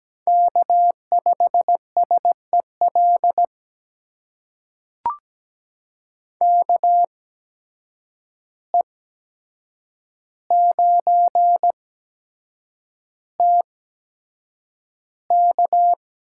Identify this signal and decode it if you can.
Morse code, heard as a keyed tone, about 17 wpm, spaced as for 7 wpm, 700 Hz. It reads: K5SEL KE9TK
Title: USA Call Signs - 5 Characters - 8x Character Spacing - 17wpm